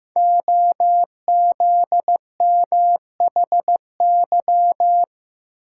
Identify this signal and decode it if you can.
Morse code, heard as a keyed tone, about 15 wpm, 700 Hz. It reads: OZMHY
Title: Code Groups - 5 Letters - 15wpm